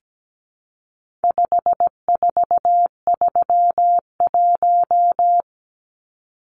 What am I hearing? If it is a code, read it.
Morse code, heard as a keyed tone, about 17 wpm, 700 Hz. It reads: 5431
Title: Numbers - 4-Digit 17wpm